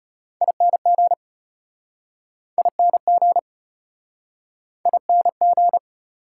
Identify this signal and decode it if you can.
Morse code, heard as a keyed tone, about 37 wpm, 690 Hz. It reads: ING ING ING